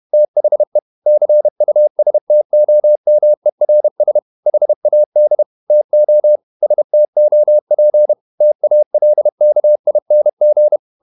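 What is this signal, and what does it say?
Morse code, heard as a keyed tone, about 31 wpm, 600 Hz. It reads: THE CUSTOMERS HAD TO STOP TALKING
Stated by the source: Morse code at 30 wpm, the novel Never by Ken Follett in English, from CW med Gnister.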